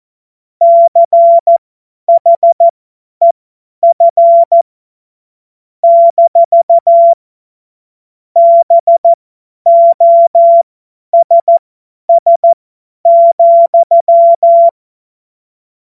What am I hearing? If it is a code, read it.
Morse code, heard as a keyed tone, about 14 wpm, spaced as for 10 wpm, 680 Hz. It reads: CHEF - BOSS,
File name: Morsekurs / Witze / Witz 24. Januar 2023 50 F70